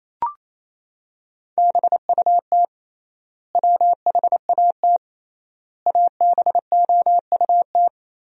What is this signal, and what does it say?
Morse code, heard as a keyed tone, about 28 wpm, 700 Hz. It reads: BUT WHAT ABOUT
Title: 3x Word-Spacing - Sentences from Top 200 Words - 28wpm